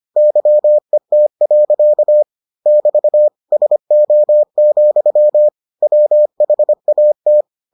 Morse code, heard as a keyed tone, about 25 wpm, 600 Hz. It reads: YET. <BT> SO, WHAT